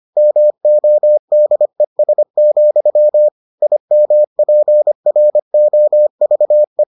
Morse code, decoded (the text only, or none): MODES, IMPROVE